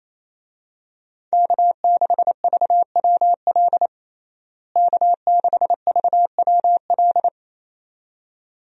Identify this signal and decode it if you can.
Morse code, heard as a keyed tone, about 28 wpm, 700 Hz. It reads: K6VWL K6VWL
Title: USA Call Signs - 5 Characters - Repeated 2x - 28wpm